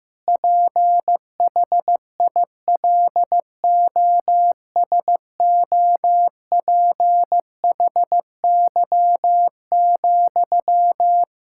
Morse code, heard as a keyed tone, about 15 wpm, 700 Hz. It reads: PHILOSOPHY,
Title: Sentences from Top 1000 Words - Review Part 6 - 15wpm